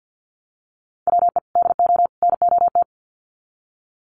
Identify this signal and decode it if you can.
Morse code, heard as a keyed tone, about 49 wpm, 700 Hz. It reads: WE DO NOT
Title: Sets of 3 Words 50wpm